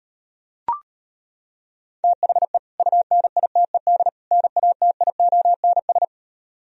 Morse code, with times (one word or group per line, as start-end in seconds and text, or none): 2.0-2.6 THE
2.8-4.1 UNITED
4.3-6.0 NATIONS